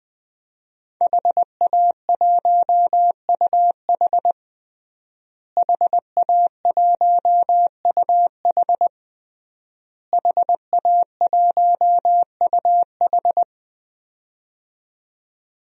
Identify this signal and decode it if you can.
Morse code, heard as a keyed tone, about 20 wpm, 700 Hz. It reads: HA1UH HA1UH HA1UH
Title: DXCC-100 Call Signs - 5 Characters - Repeated 3x - 20wpm